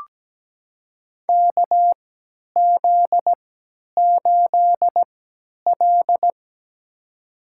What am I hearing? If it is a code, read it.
Morse code, heard as a keyed tone, about 17 wpm, spaced as for 10 wpm, 700 Hz. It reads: KZ8L
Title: USA Call Signs - 4 Characters - 3x Character Spacing - 17wpm